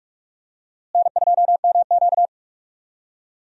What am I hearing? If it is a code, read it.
Morse code, heard as a keyed tone, about 44 wpm, 700 Hz. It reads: N2MQ
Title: USA Call Signs - 4 Characters - Repeated 3x - 45wpm